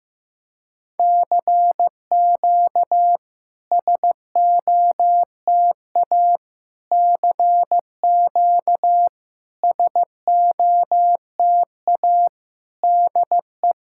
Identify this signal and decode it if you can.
Morse code, heard as a keyed tone, about 15 wpm, 700 Hz. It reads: CQ SOTA CQ SOTA DE
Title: SOTA QSO Practice 15wpm